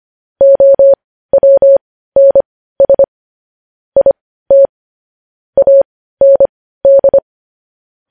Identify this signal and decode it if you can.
Morse code, heard as a keyed tone, about 25 wpm, spaced as for 15 wpm, 550 Hz. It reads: OWNS IT AND
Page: news-headlines-in-morse-code-at-15-wpm